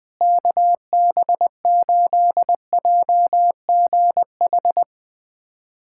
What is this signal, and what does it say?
Morse code, heard as a keyed tone, about 20 wpm, 700 Hz. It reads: KB8JGH
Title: USA Callsign DE Callsign - 6 Characters - 20wpm